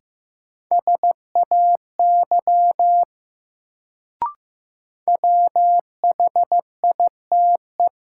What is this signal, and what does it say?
Morse code, heard as a keyed tone, about 15 wpm, 700 Hz. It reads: SAY WHITE
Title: Sets of 2 Words - New Word Drill - 15wpm